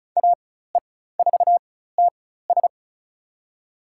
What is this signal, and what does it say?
Morse code, heard as a keyed tone, about 34 wpm, spaced as for 16 wpm, 700 Hz. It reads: AE4TS
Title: USA Call Signs - 5 Characters - 4x Character Spacing - 35wpm